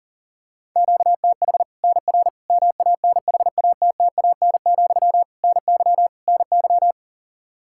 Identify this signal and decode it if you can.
Morse code, heard as a keyed tone, about 37 wpm, 700 Hz. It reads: QTH NR MANHATTAN, NY NY
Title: Practice QSO - Exchange 1 - Beginning - 40wpm